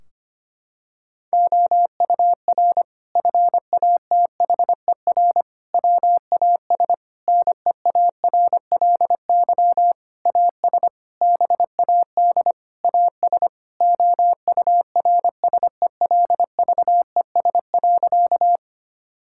Morse code, heard as a keyed tone, about 25 wpm, 700 Hz. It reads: OUR FATHER WAS NEARLY AS BAD AS OURSELVES.
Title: Sentences from Top 1600 Words 25wpm